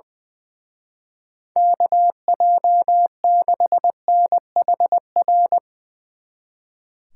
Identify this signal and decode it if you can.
Morse code, heard as a keyed tone, about 22 wpm, 700 Hz. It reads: KJ6NHR
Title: Speed Racing - USA 2x3 Call Signs 20wpm